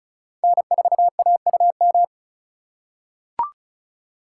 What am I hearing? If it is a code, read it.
Morse code, heard as a keyed tone, about 36 wpm, 700 Hz. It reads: N4AUM
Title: USA Call Signs - 5 Characters - 4x Character Spacing - 35wpm